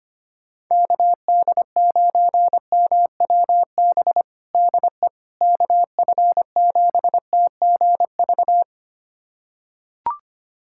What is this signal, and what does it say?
Morse code, heard as a keyed tone, about 25 wpm, 700 Hz. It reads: KD9MWB DE KF7TGV
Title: USA Callsign DE Callsign - 6 Characters - 25wpm